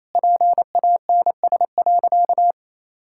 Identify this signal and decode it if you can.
Morse code, heard as a keyed tone, about 28 wpm, 700 Hz. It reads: PANS.
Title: Pangram Challenge v2 - 28wpm